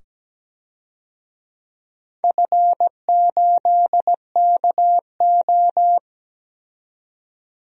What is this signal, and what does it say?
Morse code, heard as a keyed tone, about 17 wpm, 700 Hz. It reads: F8KO
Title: DXCC-100 Call Signs - 4 Characters - 17wpm